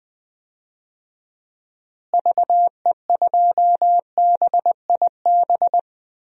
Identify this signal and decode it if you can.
Morse code, heard as a keyed tone, about 20 wpm, 700 Hz. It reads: VE2BIB
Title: DXCC-100 Call Signs - 6 Characters - Repeated 2x - 20wpm